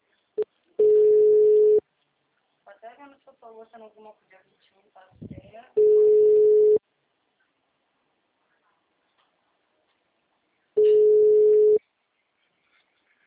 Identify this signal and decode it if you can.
Ringback tone (Europe, 425 Hz), with speech.